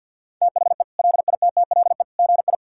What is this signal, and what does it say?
Morse code, heard as a keyed tone, about 50 wpm, 700 Hz. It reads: THE LITTLE GI
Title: Sentences from Top 1600 Words 50wpm